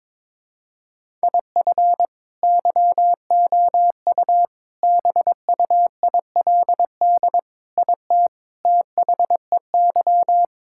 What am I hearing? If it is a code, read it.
Morse code, heard as a keyed tone, about 21 wpm, 700 Hz. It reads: IF YOU BUILD IT THEY
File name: Sentences with 7-Word Phrases - 22wpm